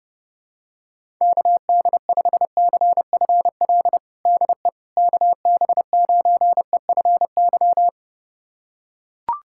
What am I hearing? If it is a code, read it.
Morse code, heard as a keyed tone, about 30 wpm, 700 Hz. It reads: KD5CFL DE KB9EFY
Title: USA Callsign DE Callsign 30wpm